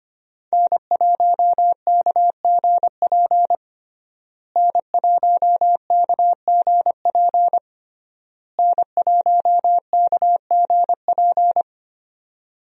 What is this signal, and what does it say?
Morse code, heard as a keyed tone, about 25 wpm, 700 Hz. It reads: N1KGP N1KGP N1KGP
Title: USA Call Signs - 5 Characters - Repeated 3x - 25wpm